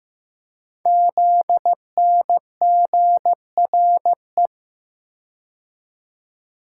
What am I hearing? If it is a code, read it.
Morse code, heard as a keyed tone, about 15 wpm, 700 Hz. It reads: ZNGRE
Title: Code Groups - 5 Letters - 15wpm